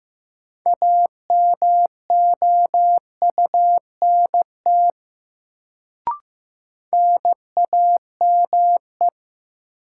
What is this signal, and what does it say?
Morse code, heard as a keyed tone, about 15 wpm, 700 Hz. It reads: AMOUNT NAME
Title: Sets of 3 Words - New Word Drill - 15wpm